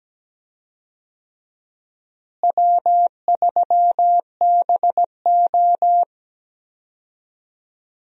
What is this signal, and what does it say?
Morse code, heard as a keyed tone, about 17 wpm, 700 Hz. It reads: W3BO
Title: USA Call Signs - 4 Characters - Repeated 2x - 17wpm